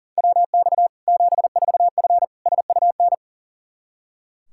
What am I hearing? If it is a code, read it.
Morse code, heard as a keyed tone, about 40 wpm, 700 Hz. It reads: WX 74F SUN
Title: Practice QSO - Exchange 2 - Info - 40wpm